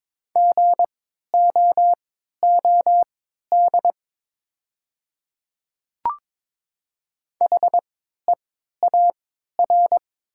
Morse code, heard as a keyed tone, about 22 wpm, spaced as for 13 wpm, 700 Hz. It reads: GOOD HEAR
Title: 3x Character Spacing - Top 300 Words - 22wpm